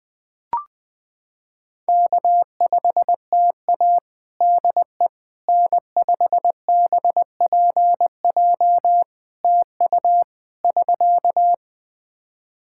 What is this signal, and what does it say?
Morse code, heard as a keyed tone, about 20 wpm, 700 Hz. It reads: K5TA DE N5BPJ TU <SK>
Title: K1USN SST QSO Practice 20wpm